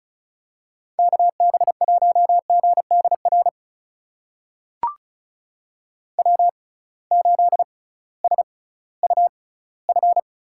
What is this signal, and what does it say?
Morse code, heard as a keyed tone, about 36 wpm, spaced as for 19 wpm, 700 Hz. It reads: KB1GDR W8SUF
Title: USA Call Signs - 6x Character Spacing - 35wpm